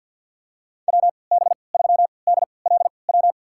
Morse code, heard as a keyed tone, about 48 wpm, spaced as for 28 wpm, 700 Hz. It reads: WB3DLW
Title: USA Call Signs - 6 Characters - 3x Character Spacing - 50wpm